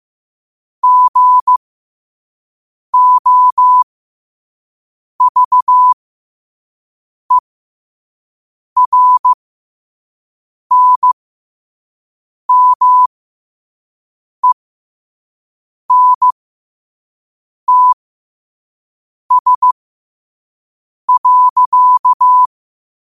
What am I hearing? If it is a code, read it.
Morse code, heard as a keyed tone, about 15 wpm, spaced as for 5 wpm, 1000 Hz. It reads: GOVERNMENTS.
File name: qotdfiles / 5wpm